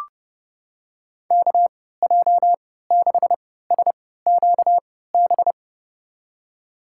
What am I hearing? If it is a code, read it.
Morse code, heard as a keyed tone, about 30 wpm, spaced as for 17 wpm, 700 Hz. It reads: KJ6SQB
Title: USA Call Signs - 6 Characters - 3x Character Spacing - 30wpm